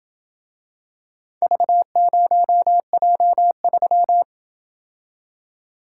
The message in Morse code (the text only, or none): V0J3